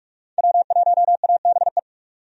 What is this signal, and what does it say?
Morse code, heard as a keyed tone, about 43 wpm, 700 Hz. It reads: W1ABE
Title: USA Call Signs - 5 Characters - 45wpm